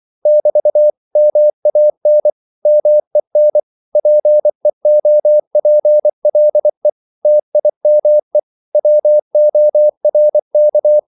Morse code, heard as a keyed tone, about 24 wpm, 600 Hz. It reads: <BT> MAN MEN PEOPLE TIME WORK
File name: CW / Practice / 100MostCommonEnglishWords24